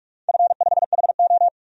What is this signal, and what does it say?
Morse code, heard as a keyed tone, about 46 wpm, 700 Hz. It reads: F5HO
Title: DXCC-100 Call Signs - 4 Characters - 45wpm